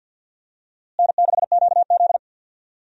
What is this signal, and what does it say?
Morse code, heard as a keyed tone, about 52 wpm, 700 Hz. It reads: N6QZ
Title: USA Call Signs - 4 Characters - Repeated 2x - 50wpm